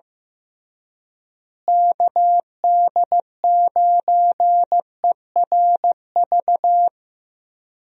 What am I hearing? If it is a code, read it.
Morse code, heard as a keyed tone, about 15 wpm, 700 Hz. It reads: KD9ERV
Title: USA Call Signs - 6 Characters - Repeated 3x - 15wpm